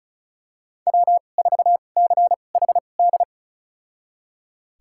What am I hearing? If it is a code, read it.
Morse code, heard as a keyed tone, about 36 wpm, spaced as for 25 wpm, 700 Hz. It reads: W4CHD
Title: USA Call Signs - 5 Characters - 2x Character Spacing - 35wpm